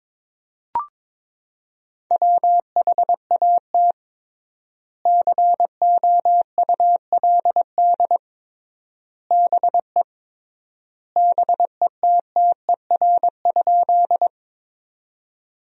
Morse code, heard as a keyed tone, about 22 wpm, 700 Hz. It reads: WHAT COULD BE BETTER?
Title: 3x Word-Spacing - Sentences from Top 400 Words - 22wpm